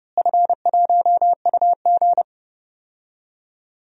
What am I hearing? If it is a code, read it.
Morse code, heard as a keyed tone, about 30 wpm, 700 Hz. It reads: F1UG
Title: DXCC-100 Call Signs - 4 Characters - Repeated 3x - 30wpm